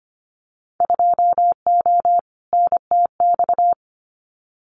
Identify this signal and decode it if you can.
Morse code, heard as a keyed tone, about 25 wpm, 700 Hz. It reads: 2O NTX
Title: Winter Field Day Exchanges 25wpm